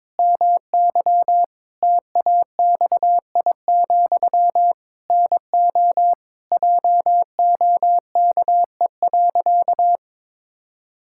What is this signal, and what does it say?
Morse code, heard as a keyed tone, about 22 wpm, 700 Hz. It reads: MY TAXI, NO JOKE.
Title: Pangram Challenge v2 - 22wpm